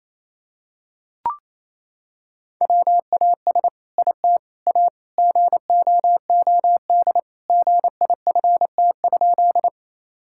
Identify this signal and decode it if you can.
Morse code, heard as a keyed tone, about 28 wpm, 700 Hz. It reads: WAS IT A GOOD GIFT?
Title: Sentences from Top 1000 Words - Review Part 6 - 28wpm